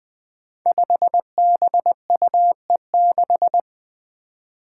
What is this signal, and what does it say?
Morse code, heard as a keyed tone, about 20 wpm, 700 Hz. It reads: 5BUE6
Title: Five Challenging Letter-Number Combinations 20wpm